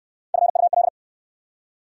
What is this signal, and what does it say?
Morse code, heard as a keyed tone, about 70 wpm, 700 Hz. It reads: FUL